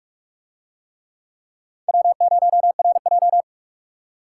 Morse code, heard as a keyed tone, about 39 wpm, 700 Hz. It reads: W0RJ